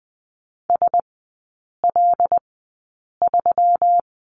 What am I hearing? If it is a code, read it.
Morse code, heard as a keyed tone, about 20 wpm, spaced as for 8 wpm, 700 Hz. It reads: SL3
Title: 121 Lesson 20e - Number 3 - Three Characters 20wpm